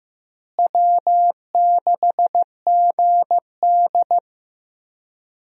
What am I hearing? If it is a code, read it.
Morse code, heard as a keyed tone, about 15 wpm, 700 Hz. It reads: W6GD
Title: USA Call Signs - 4 Characters - Repeated 2x - 15wpm